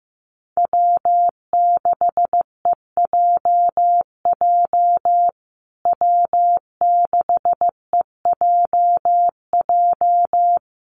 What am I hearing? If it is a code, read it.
Morse code, heard as a keyed tone, about 15 wpm, 700 Hz. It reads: W6EJJ W6EJJ
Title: USA Call Signs Repeated 3x 15wpm